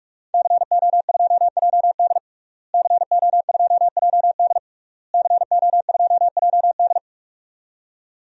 Morse code, heard as a keyed tone, about 41 wpm, 700 Hz. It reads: CO2JD CO2JD CO2JD
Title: DXCC-100 Call Signs - 5 Characters - Repeated 3x - 45wpm